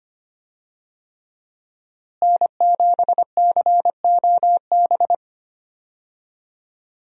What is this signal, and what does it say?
Morse code, heard as a keyed tone, about 25 wpm, 700 Hz. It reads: N7COB
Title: USA Call Signs - 5 Characters - Repeated 3x - 25wpm